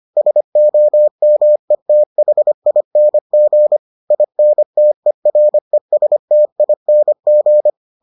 Morse code, heard as a keyed tone, about 25 wpm, 600 Hz. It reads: SOMETHING INTERESTING